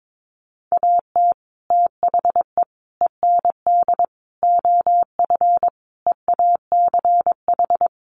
Morse code, heard as a keyed tone, about 22 wpm, 700 Hz. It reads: AT THE END OF EACH
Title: Sets of 5 Words Encore 22wpm